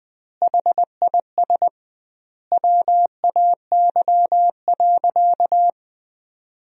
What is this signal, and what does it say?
Morse code, heard as a keyed tone, about 21 wpm, 700 Hz. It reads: HIS WAY.
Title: 2x Word-Spacing - Sentences from Top 300 Words - 20wpm